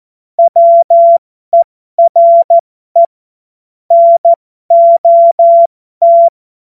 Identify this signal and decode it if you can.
Morse code, heard as a keyed tone, about 14 wpm, spaced as for 12 wpm, 680 Hz. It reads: WERE NOT